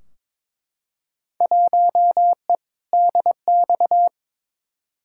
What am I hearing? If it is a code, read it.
Morse code, heard as a keyed tone, about 22 wpm, 700 Hz. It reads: 1E DX
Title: Field Day Exchanges Encore - 22wpm